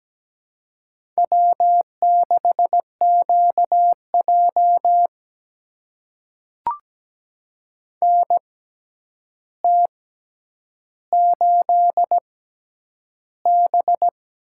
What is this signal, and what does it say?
Morse code, heard as a keyed tone, about 17 wpm, spaced as for 9 wpm, 700 Hz. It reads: W6QJ NT8B
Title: USA Call Signs - 4 Characters - 6x Character Spacing - 17wpm